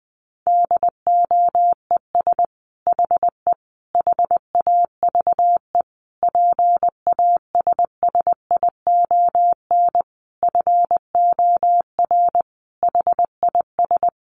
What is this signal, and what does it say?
Morse code, heard as a keyed tone, about 20 wpm, 700 Hz. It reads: DOES HE HAVE PASSION FOR HIS
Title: Sentences from Top 1300 Words 20wpm